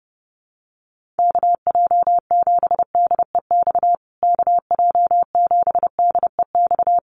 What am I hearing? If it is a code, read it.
Morse code, heard as a keyed tone, about 28 wpm, 700 Hz. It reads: KJ7DEX KJ7DEX
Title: USA Call Signs Repeated 3x 30wpm